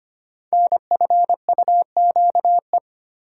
Morse code, heard as a keyed tone, about 25 wpm, 700 Hz. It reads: NFUQE
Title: Code Groups - 5 Letter:Numbers - 25wpm